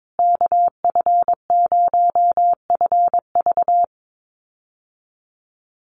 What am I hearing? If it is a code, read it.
Morse code, heard as a keyed tone, about 22 wpm, 700 Hz. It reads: KF0FV
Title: USA Call Signs Repeated 3x 22wpm